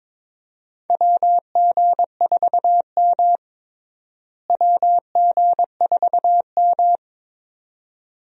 Morse code, heard as a keyed tone, about 22 wpm, 700 Hz. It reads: WG4M WG4M
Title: USA Call Signs - 4 Characters - Repeated 2x - 22wpm